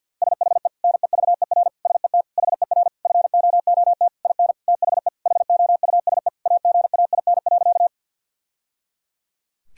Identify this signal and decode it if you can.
Morse code, heard as a keyed tone, about 50 wpm, 700 Hz. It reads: SHE NEVER SET HER FOOT IN THE HOUSE AGAIN.
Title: Sentences from Top 1000 Words - Review Part 3 - 50wpm